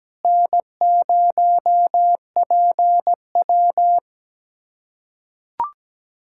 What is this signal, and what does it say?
Morse code, heard as a keyed tone, about 17 wpm, 700 Hz. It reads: N0PW
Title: USA Callsign DE Callsign - 4 Characters - 17wpm